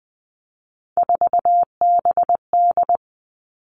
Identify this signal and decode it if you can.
Morse code, heard as a keyed tone, about 20 wpm, 700 Hz. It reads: 4BD
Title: Three Challenging Letter-Number Combinations 20wpm